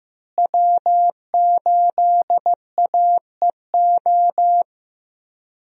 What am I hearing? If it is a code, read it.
Morse code, heard as a keyed tone, about 15 wpm, 700 Hz. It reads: W8AEO
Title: USA Call Signs - 5 Characters - Repeated 2x - 15wpm